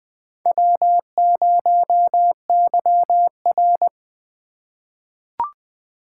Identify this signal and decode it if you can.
Morse code, heard as a keyed tone, about 20 wpm, 700 Hz. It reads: W0YR
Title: K1USN SST QSO Practice - 20wpm